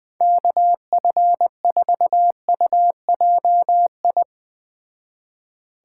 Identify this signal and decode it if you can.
Morse code, heard as a keyed tone, about 21 wpm, 700 Hz. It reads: KF4UJI